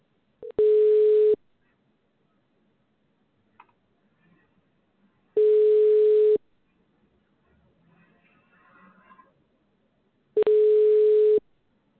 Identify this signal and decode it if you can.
Ringback tone (Europe, 425 Hz).